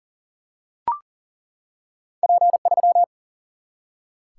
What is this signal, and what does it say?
Morse code, heard as a keyed tone, about 38 wpm, 700 Hz. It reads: P3
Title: Single-Letter Single-Number 40wpm